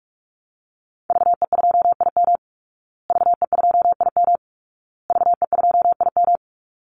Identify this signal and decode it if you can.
Morse code, heard as a keyed tone, about 44 wpm, 700 Hz. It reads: VE2IM VE2IM VE2IM
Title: Call Signs Repeated 2x - 45wpm